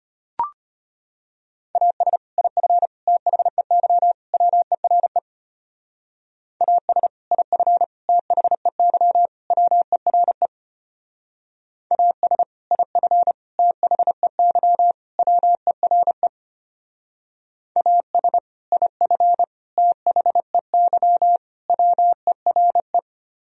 Morse code, heard as a keyed tone, about 33 wpm, 700 Hz. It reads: AS IF THEY WERE AS IF THEY WERE AS IF THEY WERE AS IF THEY WERE